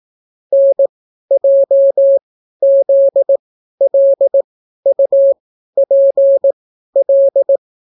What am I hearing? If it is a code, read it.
Morse code, heard as a keyed tone, about 18 wpm, spaced as for 12 wpm, 550 Hz. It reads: NJZLUPL